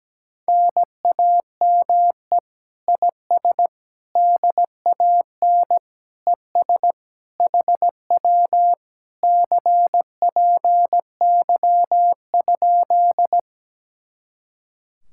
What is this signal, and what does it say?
Morse code, heard as a keyed tone, about 17 wpm, 700 Hz. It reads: NAME IS DAN ES HW CPY?
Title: Speed-Racing Common QSO Phrases - 17wpm